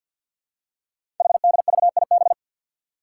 Morse code, heard as a keyed tone, about 50 wpm, 700 Hz. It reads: HDVIB
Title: Code Groups - 5 Letters - 50wpm